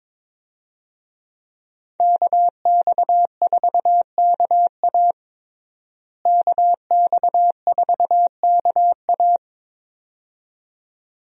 Morse code, heard as a keyed tone, about 22 wpm, 700 Hz. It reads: KX4KA KX4KA
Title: USA Call Signs - 5 Characters - Repeated 2x - 22wpm